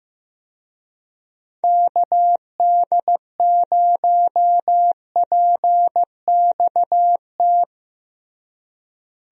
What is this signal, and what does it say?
Morse code, heard as a keyed tone, about 15 wpm, 700 Hz. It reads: KD0PXT
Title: USA Call Signs - 6 Characters - Repeated 2x - 15wpm